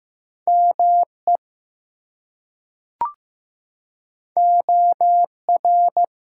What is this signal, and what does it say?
Morse code, heard as a keyed tone, about 15 wpm, 700 Hz. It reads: ME OR
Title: Two Letter Words - 15wpm